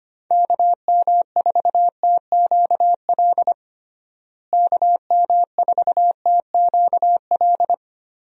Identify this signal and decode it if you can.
Morse code, heard as a keyed tone, about 25 wpm, 700 Hz. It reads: KM4TQL KM4TQL
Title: USA Call Signs - 6 Characters - Repeated 3x - 25wpm